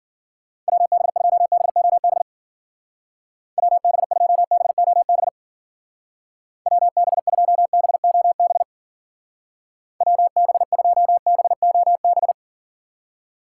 Morse code, heard as a keyed tone, about 50 wpm, 700 Hz. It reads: WB2BOB WB2BOB WB2BOB WB2BOB
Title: Speed Racing - USA 2x3 Call Signs 40wpm